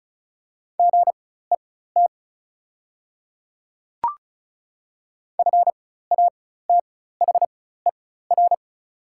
Morse code, heard as a keyed tone, about 36 wpm, spaced as for 16 wpm, 700 Hz. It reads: GET FATHER
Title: 4x Character Spacing - Top 200 Words - 35wpm